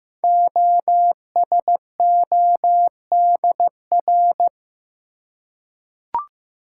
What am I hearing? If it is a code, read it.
Morse code, heard as a keyed tone, about 15 wpm, 700 Hz. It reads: OSODR
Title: Code Groups - 5 Letters - 15wpm